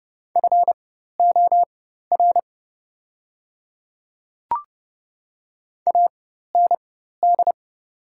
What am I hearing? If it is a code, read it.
Morse code, heard as a keyed tone, about 30 wpm, spaced as for 14 wpm, 700 Hz. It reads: FOR AND
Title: 4x Character Spacing - Top 100 Words - 30wpm